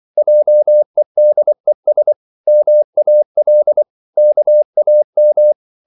Morse code, heard as a keyed tone, about 24 wpm, 600 Hz. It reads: JEDES MAL KAM